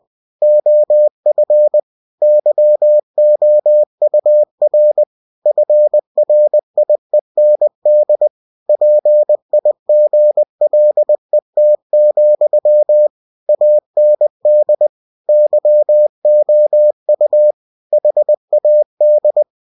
Morse code, heard as a keyed tone, about 20 wpm, 600 Hz. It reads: OF YOUR FRIEND PIGLET, AND YOU HAD
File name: cw-books / winnie-the-pooh / winnie-20wpm-page-0005